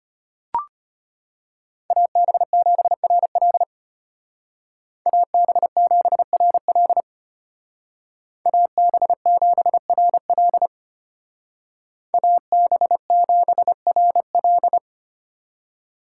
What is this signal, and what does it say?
Morse code, heard as a keyed tone, about 33 wpm, 700 Hz. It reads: AB7RL AB7RL AB7RL AB7RL